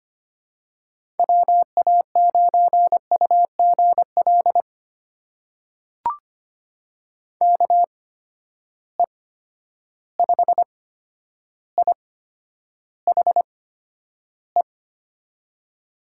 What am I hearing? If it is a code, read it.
Morse code, heard as a keyed tone, about 25 wpm, spaced as for 11 wpm, 700 Hz. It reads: WA9UGL KE5IHE
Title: USA Call Signs - 6 Characters - 8x Character Spacing - 25wpm